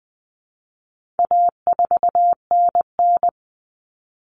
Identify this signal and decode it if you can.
Morse code, heard as a keyed tone, about 20 wpm, 700 Hz. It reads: A4NN
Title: Common Cut Numbers - 4-Digit 20wpm